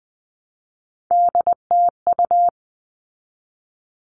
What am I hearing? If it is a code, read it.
Morse code, heard as a keyed tone, about 20 wpm, 700 Hz. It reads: DTU
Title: All Cut Numbers - 3-Digit 20wpm